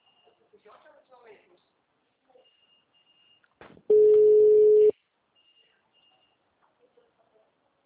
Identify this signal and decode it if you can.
Ringback tone (Europe, 425 Hz), with speech.
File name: out- -2003- - - .53169